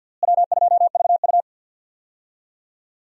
Morse code, heard as a keyed tone, about 53 wpm, 700 Hz. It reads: W2VU
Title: USA Call Signs - 4 Characters - Repeated 2x - 50wpm